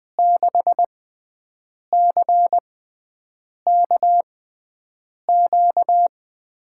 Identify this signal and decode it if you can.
Morse code, heard as a keyed tone, about 20 wpm, spaced as for 7 wpm, 700 Hz. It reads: 6CKQ
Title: USA Call Signs - 6 Characters - 6x Character Spacing - 20wpm